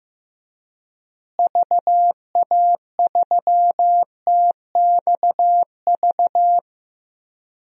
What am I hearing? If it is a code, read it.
Morse code, heard as a keyed tone, about 15 wpm, 700 Hz. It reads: VA3TXV
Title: DXCC-100 Call Signs - 6 Characters - Repeated 2x - 15wpm